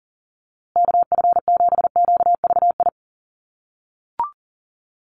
Morse code, heard as a keyed tone, about 40 wpm, 700 Hz. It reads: KF7QVI
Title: USA Call Signs Repeated 3x 40wpm